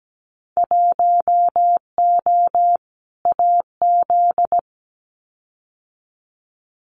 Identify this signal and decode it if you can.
Morse code, heard as a keyed tone, about 17 wpm, 700 Hz. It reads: 1O AZ
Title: Winter Field Day Exchanges 17wpm